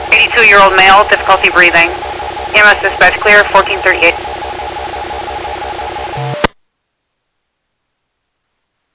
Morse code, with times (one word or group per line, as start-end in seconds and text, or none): none